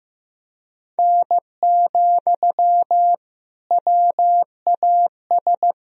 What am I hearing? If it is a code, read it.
Morse code, heard as a keyed tone, about 16 wpm, 700 Hz. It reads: N, WAS
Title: Sentences from Top 100 Words and Abbreviations 15wpm